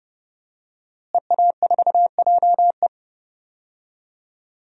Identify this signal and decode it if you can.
Morse code, heard as a keyed tone, about 30 wpm, 700 Hz. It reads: EA4JE
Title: DXCC-100 Call Signs - 5 Characters - Repeated 2x - 30wpm